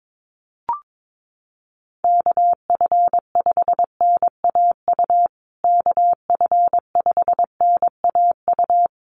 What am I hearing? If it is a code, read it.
Morse code, heard as a keyed tone, about 22 wpm, 700 Hz. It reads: KF5NAU KF5NAU
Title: USA Call Signs Repeated 2x 22wpm